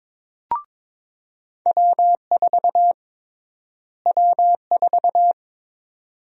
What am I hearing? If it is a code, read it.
Morse code, heard as a keyed tone, about 21 wpm, 700 Hz. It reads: W4 W4
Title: USA Call Sign Prefixes - Repeated 2x - 22wpm